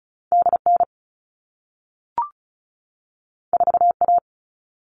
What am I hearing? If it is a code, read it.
Morse code, heard as a keyed tone, about 36 wpm, 700 Hz. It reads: DN 4A
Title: All Cut Numbers - 2-Digit 35wpm